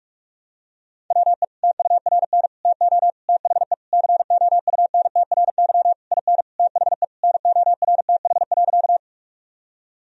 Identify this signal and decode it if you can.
Morse code, heard as a keyed tone, about 46 wpm, 700 Hz. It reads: WE TURN TO THE COUNTRY IN THE NORTH.
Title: Sentences from Top 1000 Words - Review Part 1 - 45wpm